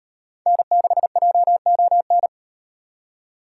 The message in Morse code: N6JON

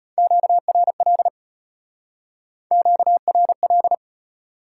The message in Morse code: QRL QRL

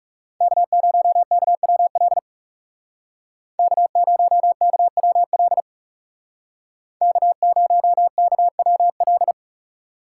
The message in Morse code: K0KWL K0KWL K0KWL

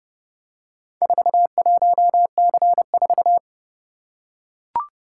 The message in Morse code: 41C4

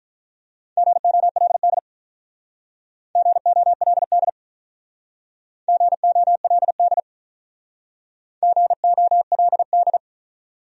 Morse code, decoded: GOLD GOLD GOLD GOLD

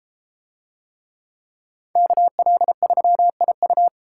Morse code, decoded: KL3IU

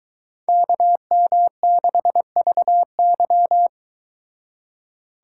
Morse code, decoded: KM6VY

